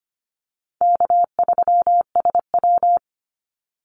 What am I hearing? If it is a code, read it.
Morse code, heard as a keyed tone, about 25 wpm, 700 Hz. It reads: K3SW
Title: Easy Call Signs - 25wpm